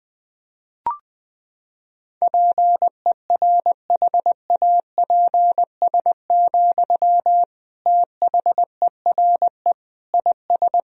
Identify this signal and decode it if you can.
Morse code, heard as a keyed tone, about 20 wpm, 700 Hz. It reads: PERHAPS, THERE IS
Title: Sentences from Top 1000 Words - Review Part 4 - 20wpm